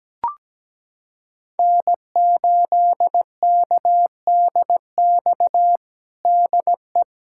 Morse code, heard as a keyed tone, about 17 wpm, 700 Hz. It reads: N8KDX DE
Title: USA Callsign DE Callsign - 5 Characters - 17wpm